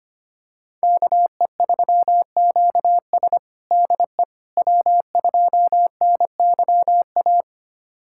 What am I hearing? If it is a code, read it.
Morse code, heard as a keyed tone, about 25 wpm, 700 Hz. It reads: KE3QS DE W2NYA
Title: USA Callsign DE Callsign - 5 Characters - 25wpm